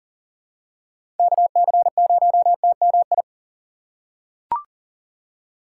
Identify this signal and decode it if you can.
Morse code, heard as a keyed tone, about 40 wpm, 700 Hz. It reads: KC0TMI
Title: USA Call Signs - 6 Characters - 6x Character Spacing - 40wpm